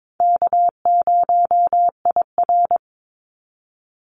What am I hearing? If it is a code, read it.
Morse code, heard as a keyed tone, about 22 wpm, 700 Hz. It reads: K0IR
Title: Easy Call Signs - 22wpm